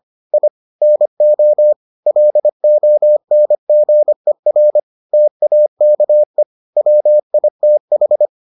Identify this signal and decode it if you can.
Morse code, heard as a keyed tone, about 25 wpm, 600 Hz. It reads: I NO LONGER TAKE WITH